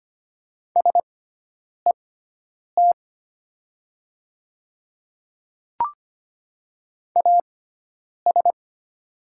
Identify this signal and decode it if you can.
Morse code, heard as a keyed tone, about 25 wpm, spaced as for 9 wpm, 700 Hz. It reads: SET AS
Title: 6x Character Spacing - Top 200 Words - 25wpm